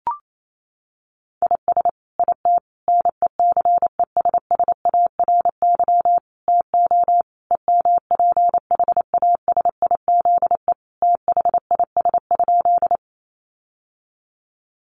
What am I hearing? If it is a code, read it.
Morse code, heard as a keyed tone, about 28 wpm, 700 Hz. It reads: IS IT NECESSARY TO EMPHASIZE THIS?
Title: Sentences from Top 1500 Words 28wpm